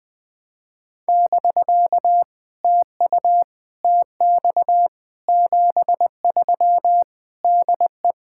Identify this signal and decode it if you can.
Morse code, heard as a keyed tone, about 19 wpm, 700 Hz. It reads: <BK> TU TX 73 DE
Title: POTA QSO Practice - 20wpm